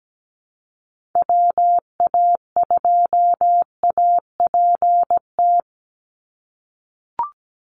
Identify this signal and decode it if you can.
Morse code, heard as a keyed tone, about 17 wpm, 700 Hz. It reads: WA2APT
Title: USA Call Signs 17wpm